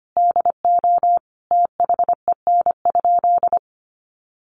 Morse code, heard as a keyed tone, about 25 wpm, 700 Hz. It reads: DO THEN?
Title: Sentences from Top 100 Words 25wpm